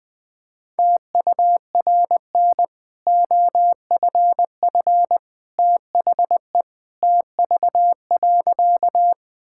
Morse code, heard as a keyed tone, about 20 wpm, 700 Hz. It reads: TURN OFF THE TV.
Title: Easy Sentences with Hints - 20wpm